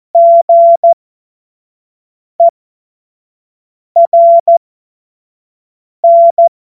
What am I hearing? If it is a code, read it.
Morse code, heard as a keyed tone, about 14 wpm, spaced as for 5 wpm, 680 Hz. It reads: GERN